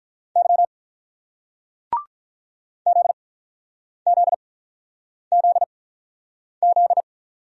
Morse code, heard as a keyed tone, about 44 wpm, spaced as for 9 wpm, 700 Hz. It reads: Y ZZZZ